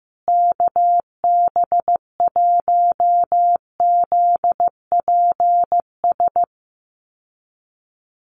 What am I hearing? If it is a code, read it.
Morse code, heard as a keyed tone, about 14 wpm, 700 Hz. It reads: KB1ZPS
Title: USA Call Signs Repeated 2x 15wpm